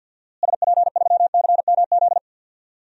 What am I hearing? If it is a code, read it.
Morse code, heard as a keyed tone, about 47 wpm, 700 Hz. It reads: SP3CMZ
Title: DXCC-100 Call Signs - 6 Characters - Repeated 2x - 50wpm